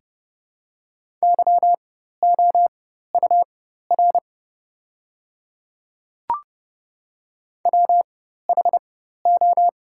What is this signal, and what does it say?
Morse code, heard as a keyed tone, about 30 wpm, spaced as for 14 wpm, 690 Hz. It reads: YOUR WHO
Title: 4x Character Spacing - Top 100 Words - 30wpm